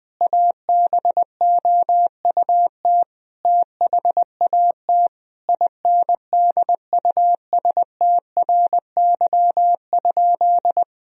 Morse code, heard as a keyed tone, about 21 wpm, 700 Hz. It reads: ABOUT THAT INDUSTRY?
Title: Sentences from Top 1000 Words - Review Part 4 - 20wpm